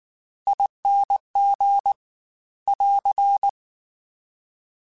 Morse code, heard as a keyed tone, about 19 wpm, 800 Hz. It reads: ING <AR>
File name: Special INGs / Special INGs-mp3s / Slow ing @18